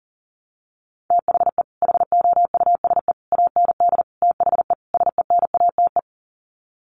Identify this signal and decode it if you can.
Morse code, heard as a keyed tone, about 40 wpm, 700 Hz. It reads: THE HOUSE AND THE SENATE
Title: Sets of 5 Words Encore2 40wpm